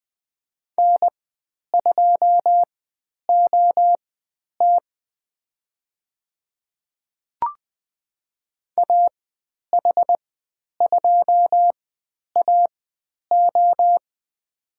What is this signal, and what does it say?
Morse code, heard as a keyed tone, about 20 wpm, spaced as for 10 wpm, 700 Hz. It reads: N2OT AH2AO
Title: 095 Lesson 16g - Letter c - Call Signs 20wpm